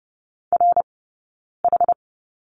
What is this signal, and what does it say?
Morse code, heard as a keyed tone, about 30 wpm, spaced as for 9 wpm, 690 Hz. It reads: RH
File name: Lesson 18d - Letter m - Two Characters 30wpm